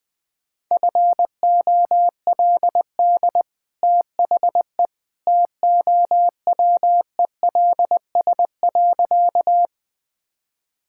Morse code, heard as a keyed tone, about 21 wpm, 700 Hz. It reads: FOLD THE TOWELS.
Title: Easy Sentences with Hints - 20wpm